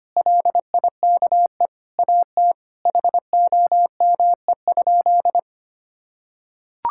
Morse code, heard as a keyed tone, about 25 wpm, 700 Hz. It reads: LIKE AT HOME?
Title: Sentences from Top 1000 Words - Review Part 5 - 25wpm